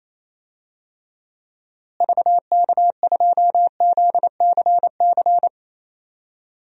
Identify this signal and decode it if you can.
Morse code, heard as a keyed tone, about 27 wpm, 700 Hz. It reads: VK2ZCC
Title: DXCC-100 Call Signs - 6 Characters - Repeated 2x - 28wpm